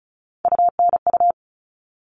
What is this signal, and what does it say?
Morse code, heard as a keyed tone, about 34 wpm, 700 Hz. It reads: UNU
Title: All Cut Numbers - 3-Digit 35wpm